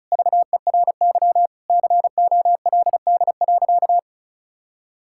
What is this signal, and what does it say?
Morse code, heard as a keyed tone, about 34 wpm, 700 Hz. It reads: VERY COLD.
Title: Sentences from Top 1000 Words - Review Part 5 - 35wpm